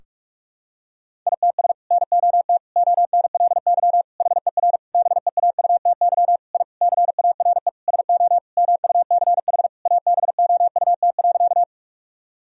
Morse code, heard as a keyed tone, about 41 wpm, 700 Hz. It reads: ITS NOT ONLY HER BEAUTY I CARE SO MUCH ABOUT.
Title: Sentences from Top 1000 Words - Review Part 2 - 45wpm